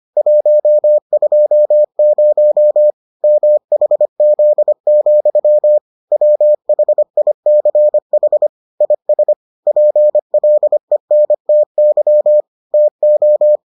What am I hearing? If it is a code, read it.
Morse code, heard as a keyed tone, about 25 wpm, 600 Hz. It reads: 120 MHZ, WHICH IS PLENTY TO